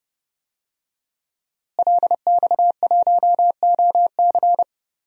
Morse code, heard as a keyed tone, about 30 wpm, 700 Hz. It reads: LX1OC
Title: DXCC-100 Call Signs - 5 Characters - Repeated 2x - 30wpm